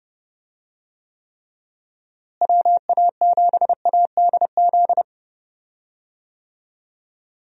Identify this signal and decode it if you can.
Morse code, heard as a keyed tone, about 31 wpm, 700 Hz. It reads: WA7ADZ
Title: USA Call Signs - 6 Characters - Repeated 2x - 30wpm